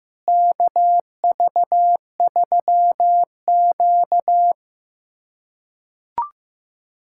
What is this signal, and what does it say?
Morse code, heard as a keyed tone, about 15 wpm, 700 Hz. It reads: KV3Q
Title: USA Callsign DE Callsign - 4 Characters - 15wpm